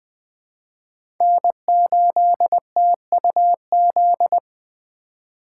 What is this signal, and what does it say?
Morse code, heard as a keyed tone, about 20 wpm, 700 Hz. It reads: N8TUZ
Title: USA Call Signs - 5 Characters - 2x Character Spacing - 20wpm